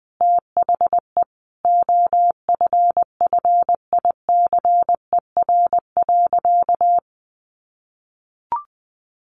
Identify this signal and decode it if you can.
Morse code, heard as a keyed tone, about 20 wpm, 700 Hz. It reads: THE OFFICER.
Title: Sentences from Top 1200 Words 20wpm